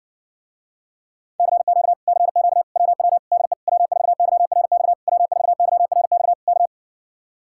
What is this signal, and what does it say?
Morse code, heard as a keyed tone, about 70 wpm, 700 Hz. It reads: CQ CQ WW DE W4YRX W4YRX K